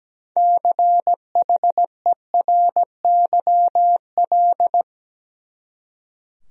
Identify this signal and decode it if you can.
Morse code, heard as a keyed tone, about 17 wpm, 700 Hz. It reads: CHERYL
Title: USA Common Names - 17wpm